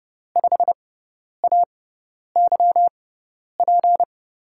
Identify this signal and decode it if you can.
Morse code, heard as a keyed tone, about 30 wpm, spaced as for 10 wpm, 700 Hz. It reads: 5AYP
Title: USA Call Signs - 6 Characters - 6x Character Spacing - 30wpm